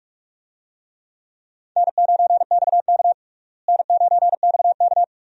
Morse code, heard as a keyed tone, about 41 wpm, 700 Hz. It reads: N9XK N9XK
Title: USA Call Signs - 4 Characters - Repeated 2x - 45wpm